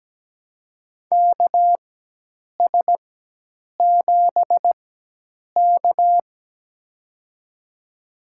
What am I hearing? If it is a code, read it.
Morse code, heard as a keyed tone, about 17 wpm, spaced as for 8 wpm, 700 Hz. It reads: KS7K
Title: USA Call Signs - 4 Characters - 4x Character Spacing - 17wpm